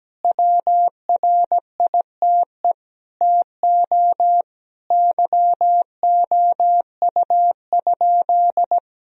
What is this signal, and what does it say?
Morse code, heard as a keyed tone, about 17 wpm, 700 Hz. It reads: WRITE TO YOU?
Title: Sentences from Top 1000 Words - Review Part 1 - 17wpm